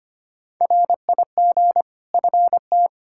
Morse code, heard as a keyed tone, about 25 wpm, 700 Hz. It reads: RIG FT
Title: Practice QSO - Exchange 2 - Info - 25wpm